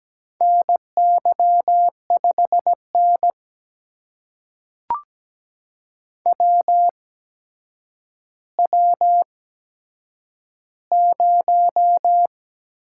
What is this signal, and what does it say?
Morse code, heard as a keyed tone, about 17 wpm, spaced as for 8 wpm, 700 Hz. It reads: NY5N WW0
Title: USA Call Signs - 4 Characters - 8x Character Spacing - 17wpm